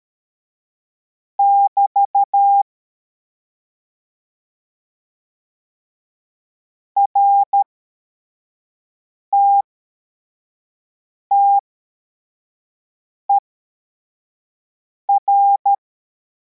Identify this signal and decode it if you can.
Morse code, heard as a keyed tone, about 13 wpm, spaced as for 4 wpm, 800 Hz. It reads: <BT> RTTER